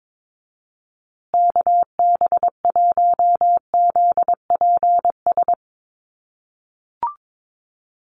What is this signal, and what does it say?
Morse code, heard as a keyed tone, about 22 wpm, 700 Hz. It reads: KB1ZPS
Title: USA Call Signs Repeated 2x 22wpm